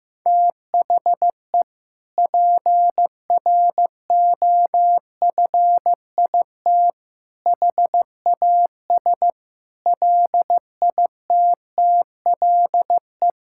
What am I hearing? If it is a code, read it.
Morse code, heard as a keyed tone, about 16 wpm, 700 Hz. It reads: THE PROFIT HAS LITTLE
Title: Sentences from Top 1000 Words - Review Part 5 - 15wpm